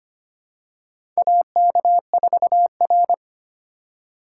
Morse code, heard as a keyed tone, about 25 wpm, 700 Hz. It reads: AK4R
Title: USA Call Signs - 4 Characters - 4x Character Spacing - 25wpm